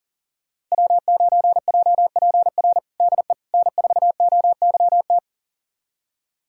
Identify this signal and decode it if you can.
Morse code, heard as a keyed tone, about 40 wpm, 700 Hz. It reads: W9JPR DE N4OYT
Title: USA Callsign DE Callsign - 5 Characters - 40wpm